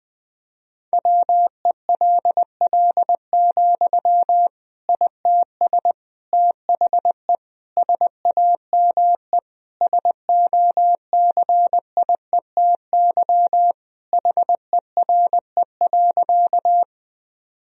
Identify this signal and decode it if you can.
Morse code, heard as a keyed tone, about 20 wpm, 700 Hz. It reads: WELL, ITS THE SAME SOCIETY HERE.
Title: Sentences from Top 1000 Words - Review Part 4 - 20wpm